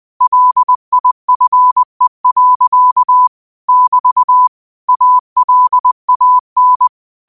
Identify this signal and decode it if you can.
Morse code, heard as a keyed tone, about 20 wpm, 1000 Hz. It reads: LIFE. <BT> ALAN
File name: qotdfiles / 20wpm